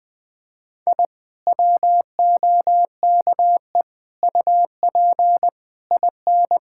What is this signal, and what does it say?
Morse code, heard as a keyed tone, about 20 wpm, 700 Hz. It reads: I WOKE UP IN
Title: Sentences with 5-Word Phrases - 20wpm